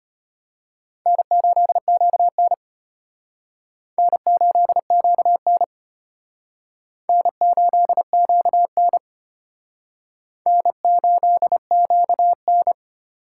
Morse code, heard as a keyed tone, about 33 wpm, 700 Hz. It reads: N8QN N8QN N8QN N8QN